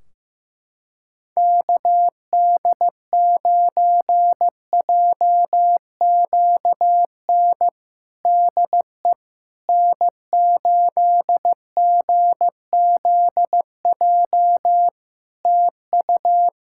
Morse code, heard as a keyed tone, about 15 wpm, 700 Hz. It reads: KD9JQN DE N8GZJ TU
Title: K1USN SST QSO Practice 15wpm